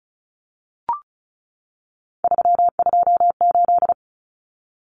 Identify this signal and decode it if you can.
Morse code, heard as a keyed tone, about 34 wpm, 700 Hz. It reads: 328
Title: Common Cut Numbers - 3-Digit 35wpm